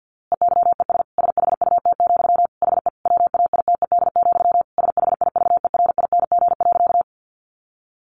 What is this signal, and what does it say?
Morse code, heard as a keyed tone, about 50 wpm, 700 Hz. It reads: EYES SHUT, HE WAITED, SHIVERING.
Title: A Sound Of Thunder 50wpm